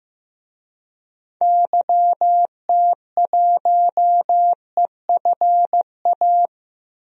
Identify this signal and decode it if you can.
Morse code, heard as a keyed tone, about 16 wpm, 700 Hz. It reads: YT1EFA
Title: DXCC-100 Call Signs - 6 Characters - Repeated 2x - 15wpm